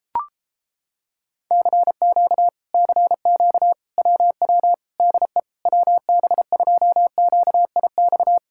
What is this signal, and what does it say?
Morse code, heard as a keyed tone, about 33 wpm, 700 Hz. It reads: CQ CQ WW DE WB2QIX